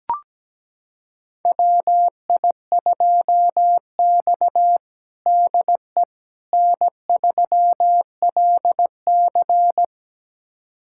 Morse code, heard as a keyed tone, about 17 wpm, 700 Hz. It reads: WI2X DE N3LC
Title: USA Callsign DE Callsign - 4 Characters - 17wpm